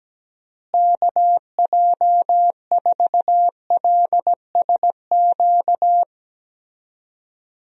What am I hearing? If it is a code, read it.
Morse code, heard as a keyed tone, about 17 wpm, 700 Hz. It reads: KJ4LSQ
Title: USA Call Signs - 6 Characters - Repeated 2x - 17wpm